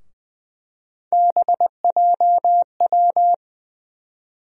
Morse code, heard as a keyed tone, about 20 wpm, 700 Hz. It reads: BJW
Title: USA Call Sign Postfixes - 3 Letters 20wpm